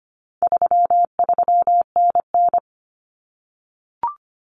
Morse code, heard as a keyed tone, about 25 wpm, 700 Hz. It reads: 33NN
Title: Common Cut Numbers - 4-Digit 25wpm